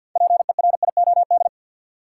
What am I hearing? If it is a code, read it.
Morse code, heard as a keyed tone, about 53 wpm, 700 Hz. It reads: PERIOD
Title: Sentences from Top 800 Words - New Word Drill - 50wpm